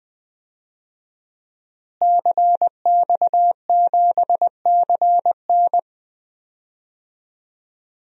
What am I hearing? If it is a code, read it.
Morse code, heard as a keyed tone, about 20 wpm, 700 Hz. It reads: CX7CN
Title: DXCC-100 Call Signs - 5 Characters - Repeated 2x - 20wpm